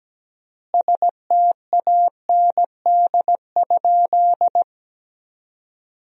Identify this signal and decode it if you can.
Morse code, heard as a keyed tone, about 17 wpm, 700 Hz. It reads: STAND?
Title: 3x Word-Spacing - Sentences from Top 300 Words - 17wpm